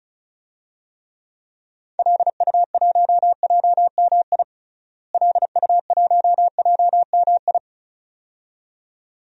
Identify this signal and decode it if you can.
Morse code, heard as a keyed tone, about 34 wpm, 700 Hz. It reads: LU1JMI LU1JMI
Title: DXCC-100 Call Signs - 6 Characters - Repeated 2x - 35wpm